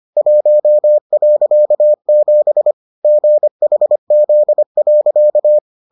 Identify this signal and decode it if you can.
Morse code, heard as a keyed tone, about 25 wpm, 600 Hz. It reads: 1.7 GHZ.